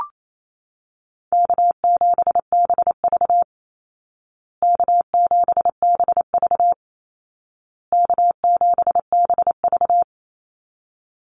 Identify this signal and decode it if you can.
Morse code, heard as a keyed tone, about 28 wpm, 700 Hz. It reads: K7BV K7BV K7BV
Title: Easy Call Signs Repeated 3x - 28wpm